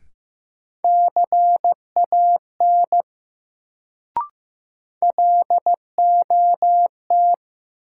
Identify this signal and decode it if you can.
Morse code, heard as a keyed tone, about 16 wpm, 700 Hz. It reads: CAN LOT
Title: Sets of 2 Words - New Word Drill - 15wpm